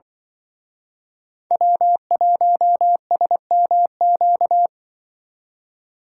Morse code, heard as a keyed tone, about 24 wpm, 700 Hz. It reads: W1SMQ
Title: Speed Racing - USA 1x3 Call Signs 20wpm